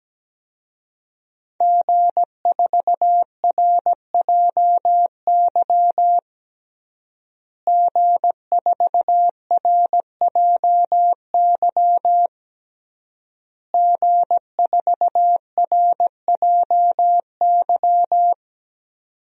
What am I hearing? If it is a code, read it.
Morse code, heard as a keyed tone, about 17 wpm, 700 Hz. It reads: G4RJY G4RJY G4RJY
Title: DXCC-100 Call Signs - 5 Characters - Repeated 3x - 17wpm